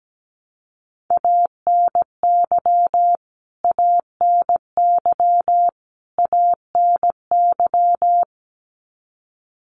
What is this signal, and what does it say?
Morse code, heard as a keyed tone, about 17 wpm, 700 Hz. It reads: ANY ANY ANY
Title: Top 200 Words Repeated 17wpm